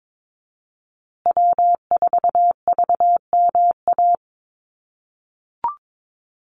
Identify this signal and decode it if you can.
Morse code, heard as a keyed tone, about 22 wpm, 700 Hz. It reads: W4VMA
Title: USA Call Signs Repeated 2x 22wpm